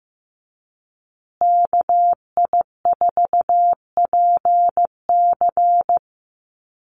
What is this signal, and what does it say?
Morse code, heard as a keyed tone, about 15 wpm, 700 Hz. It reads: KI4PC
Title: USA Call Signs Repeated 3x 15wpm